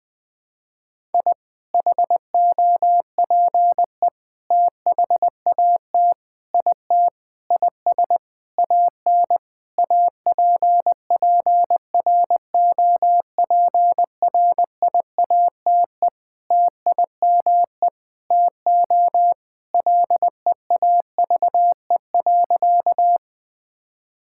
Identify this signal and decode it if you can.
Morse code, heard as a keyed tone, about 20 wpm, 700 Hz. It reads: I HOPE THAT IT IS AN APPROPRIATE TIME TO LEAVE.
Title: Sentences from Top 1000 Words - Review Part 5 - 20wpm